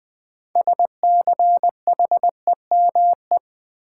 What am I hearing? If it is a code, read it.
Morse code, heard as a keyed tone, about 20 wpm, 700 Hz. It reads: SCHEME
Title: Six Letter Words - 20wpm